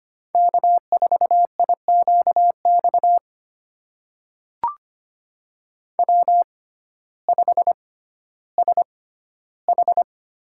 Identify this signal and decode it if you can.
Morse code, heard as a keyed tone, about 25 wpm, spaced as for 14 wpm, 700 Hz. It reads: K4IQX W5SH
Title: USA Call Signs - 6x Character Spacing - 25wpm